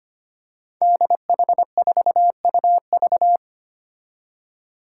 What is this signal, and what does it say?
Morse code, heard as a keyed tone, about 25 wpm, 700 Hz. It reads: DH4UV
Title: Five Challenging Letter-Number Combinations 25wpm